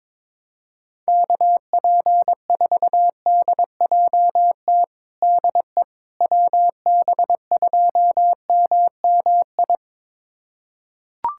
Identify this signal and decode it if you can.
Morse code, heard as a keyed tone, about 22 wpm, 700 Hz. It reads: KP4DJT DE WB2MMI
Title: USA Callsign DE Callsign - 6 Characters - 22wpm